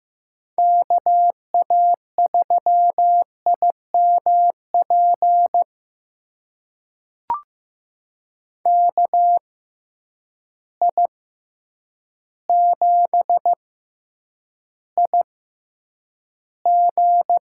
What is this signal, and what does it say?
Morse code, heard as a keyed tone, about 15 wpm, spaced as for 8 wpm, 700 Hz. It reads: KA3IMP KI7IG
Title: USA Call Signs - 6x Character Spacing - 15wpm